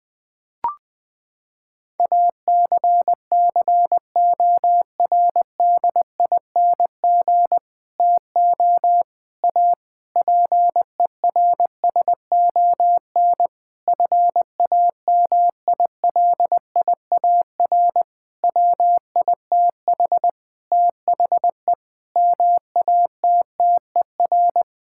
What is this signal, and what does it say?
Morse code, heard as a keyed tone, about 21 wpm, 700 Hz. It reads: ACCORDING TO A PERSON FAMILIAR WITH THE MATTER
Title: Sentences with 7-Word Phrases - 20wpm